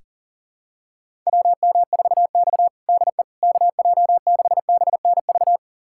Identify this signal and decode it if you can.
Morse code, heard as a keyed tone, about 40 wpm, 700 Hz. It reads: WM4X DE KJ6BNV
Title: USA Callsign DE Callsign 40wpm